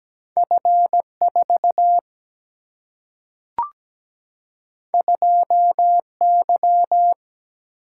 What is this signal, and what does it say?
Morse code, heard as a keyed tone, about 17 wpm, 700 Hz. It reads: F4 2Y
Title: Single-Letter Single-Number 17wpm